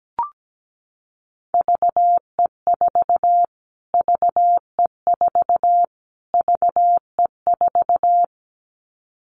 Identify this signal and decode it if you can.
Morse code, heard as a keyed tone, about 17 wpm, 700 Hz. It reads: VE4 VE4 VE4
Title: Callsign Prefixes for Top 25 DXCC - Repeated 3x - 17wpm